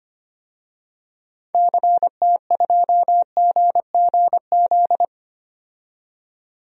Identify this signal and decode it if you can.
Morse code, heard as a keyed tone, about 25 wpm, 690 Hz. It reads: CT2GGZ
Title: DXCC-100 Call Signs - 6 Characters - 25wpm